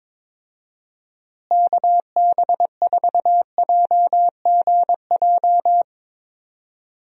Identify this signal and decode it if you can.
Morse code, heard as a keyed tone, about 22 wpm, 700 Hz. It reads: KB4JGJ
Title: USA Call Signs - 6 Characters - Repeated 3x - 22wpm